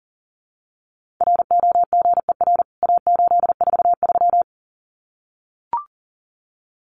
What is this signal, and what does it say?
Morse code, heard as a keyed tone, about 40 wpm, 700 Hz. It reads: ROGER A843
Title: CWT Exchange 40wpm